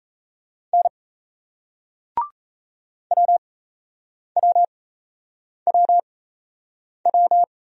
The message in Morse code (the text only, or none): N WWWW